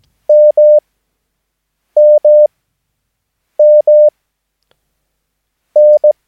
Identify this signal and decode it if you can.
Morse code, heard as a keyed tone, about 17 wpm, spaced as for 6 wpm, 600 Hz. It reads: MMMN
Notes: Övningsmaterial med alfabetet A till Ö samt siffror 0 till 9 enligt International/Continental Morse Code.